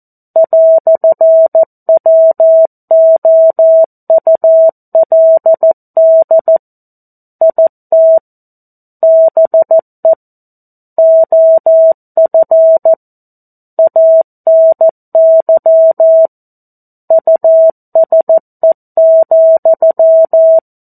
Morse code, heard as a keyed tone, about 14 wpm, 650 Hz. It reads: "WOULD IT BE OF ANY USE,
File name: 02-Chapter-02 14 WPM